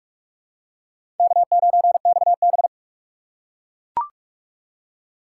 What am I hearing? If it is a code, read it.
Morse code, heard as a keyed tone, about 43 wpm, 700 Hz. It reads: K9XB
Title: USA Call Signs - 4 Characters - 4x Character Spacing - 45wpm